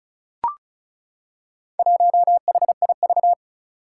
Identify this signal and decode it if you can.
Morse code, heard as a keyed tone, about 36 wpm, 700 Hz. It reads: 1HIV
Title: Four Challenging Letter-Number Combinations 35wpm